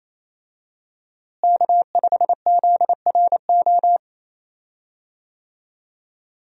Morse code, heard as a keyed tone, about 28 wpm, 700 Hz. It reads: K5ZRO
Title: USA Call Signs - 5 Characters - Repeated 3x - 28wpm